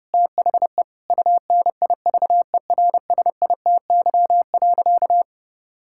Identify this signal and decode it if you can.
Morse code, heard as a keyed tone, about 30 wpm, 700 Hz. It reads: THE UNIVERSITY.
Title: Sentences from Top 1000 Words - Review Part 5 - 30wpm